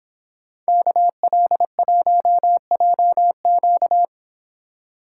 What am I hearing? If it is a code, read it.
Morse code, heard as a keyed tone, about 25 wpm, 700 Hz. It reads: KL1JQ